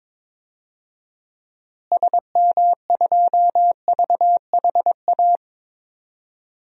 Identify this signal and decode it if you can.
Morse code, heard as a keyed tone, about 22 wpm, 700 Hz. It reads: SM2VHA
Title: DXCC-100 Call Signs - 6 Characters - Repeated 2x - 22wpm